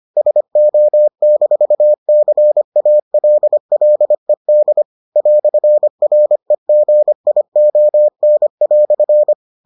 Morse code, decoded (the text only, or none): SO-CALLED "REGION"